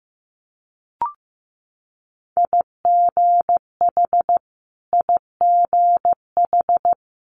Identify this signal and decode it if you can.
Morse code, heard as a keyed tone, about 15 wpm, 700 Hz. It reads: IGH IGH
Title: Multi-letter Phonemes Repeated 3x 15wpm